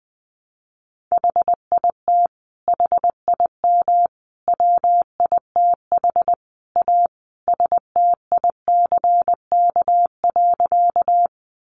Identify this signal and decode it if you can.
Morse code, heard as a keyed tone, about 20 wpm, 700 Hz. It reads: HIT HIM WITH A STICK.
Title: Sentences from Top 1200 Words 20wpm